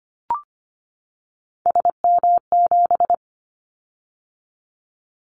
Morse code, heard as a keyed tone, about 25 wpm, 700 Hz. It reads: SM7
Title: Callsign Prefixes for Top 25 DXCC - 25wpm